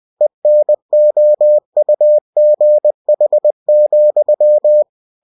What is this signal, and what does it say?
Morse code, heard as a keyed tone, about 20 wpm, 600 Hz. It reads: ENOUGH,
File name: cw-books / winnie-the-pooh / winnie-20wpm-page-0107